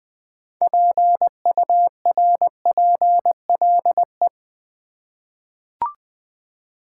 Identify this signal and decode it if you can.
Morse code, heard as a keyed tone, about 20 wpm, 700 Hz. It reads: PURPLE
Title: Six Letter Words - 20wpm